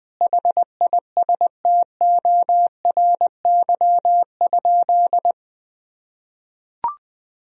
Morse code, heard as a keyed tone, about 20 wpm, 700 Hz. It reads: HISTORY?
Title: Sentences from Top 1000 Words - Review Part 4 - 20wpm